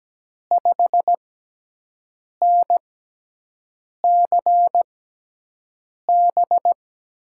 Morse code, heard as a keyed tone, about 17 wpm, spaced as for 6 wpm, 700 Hz. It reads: 5NCB
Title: USA Call Signs - 6 Characters - 6x Character Spacing - 17wpm